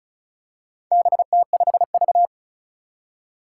Morse code, heard as a keyed tone, about 36 wpm, 700 Hz. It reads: BT5V
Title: Four Challenging Letter-Number Combinations 35wpm